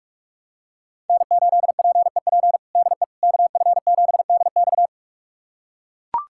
Morse code, heard as a keyed tone, about 41 wpm, 700 Hz. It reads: N8PEP DE KF7DX
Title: USA Callsign DE Callsign - 5 Characters - 45wpm